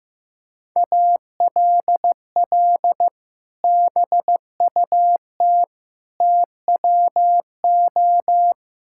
Morse code, heard as a keyed tone, about 15 wpm, 700 Hz. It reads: ALL BUT TWO
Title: Sentences from Top 1000 Words - Review Part 5 - 15wpm